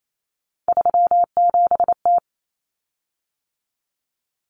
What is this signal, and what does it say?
Morse code, heard as a keyed tone, about 28 wpm, 700 Hz. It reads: 37T
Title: Common Cut Numbers - 3-Digit 28wpm